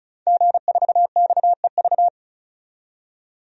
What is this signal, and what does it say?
Morse code, heard as a keyed tone, about 35 wpm, 700 Hz. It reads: G4XEV
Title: DXCC-100 Call Signs - 5 Characters - 35wpm